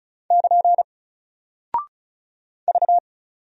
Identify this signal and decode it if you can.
Morse code, heard as a keyed tone, about 36 wpm, 700 Hz. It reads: <KN> V
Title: Single Character (Letter:Number:Punctuation:Prosign) Alt 35wpm